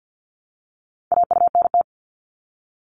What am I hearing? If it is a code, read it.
Morse code, heard as a keyed tone, about 50 wpm, 700 Hz. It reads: AUNT
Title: Multi-letter Phonemes Repeated 3x 50wpm